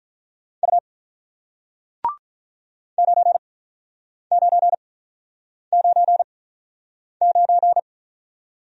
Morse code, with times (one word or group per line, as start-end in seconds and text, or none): 0.6-0.8 U
3.0-7.8 9999